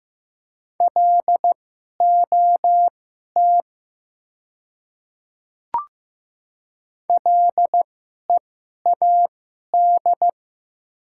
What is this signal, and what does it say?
Morse code, heard as a keyed tone, about 15 wpm, spaced as for 11 wpm, 700 Hz. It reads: LOT LEAD
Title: 2x Character Spacing - Top 500 Words - 15wpm